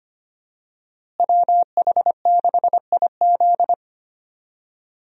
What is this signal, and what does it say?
Morse code, heard as a keyed tone, about 25 wpm, 700 Hz. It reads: WH6IZ
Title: USA Call Signs - 5 Characters - Repeated 2x - 25wpm